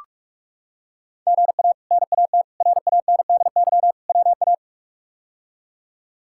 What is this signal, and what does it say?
Morse code, heard as a keyed tone, about 47 wpm, 700 Hz. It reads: GA NAT RANDY WA
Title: K1USN SST QSO Practice - 45wpm